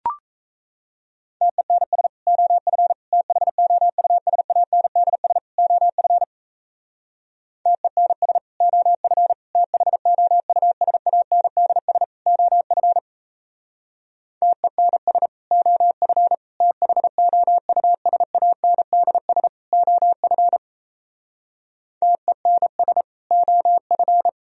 Morse code, heard as a keyed tone, about 34 wpm, 700 Hz. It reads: TENS OF THOUSANDS OF TENS OF THOUSANDS OF TENS OF THOUSANDS OF TENS OF
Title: Speed-Racing - Sets of 4 Words 28wpm